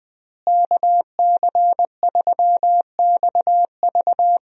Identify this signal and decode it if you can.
Morse code, heard as a keyed tone, about 20 wpm, 700 Hz. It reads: KC3XV
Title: USA Call Signs - 5 Characters - Repeated 2x - 20wpm